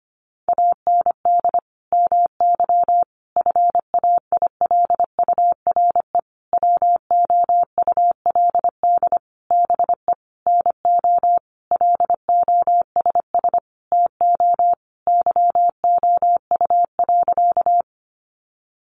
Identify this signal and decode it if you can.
Morse code, heard as a keyed tone, about 25 wpm, 700 Hz. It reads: AND MY FAILURE WOULD BE NO LOSS TO YOU.
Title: Sentences from Top 1400 Words 25wpm